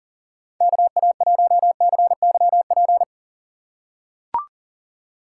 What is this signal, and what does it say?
Morse code, heard as a keyed tone, about 39 wpm, 700 Hz. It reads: KA1CYP
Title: USA Call Signs - 3x Character Spacing - 40wpm